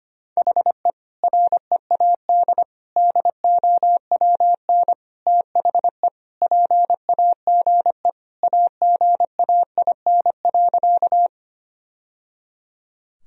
Morse code, heard as a keyed tone, about 25 wpm, 700 Hz. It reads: HE READ DOWN THE PAGE AGAIN.
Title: Sentences from Top 1000 Words - Review Part 1 - 25wpm